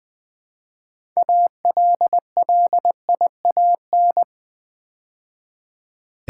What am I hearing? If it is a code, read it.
Morse code, heard as a keyed tone, about 20 wpm, 700 Hz. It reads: ALLIAN
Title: Build A Word - 2 Letters At A Time - 8-Letter Words - 20wpm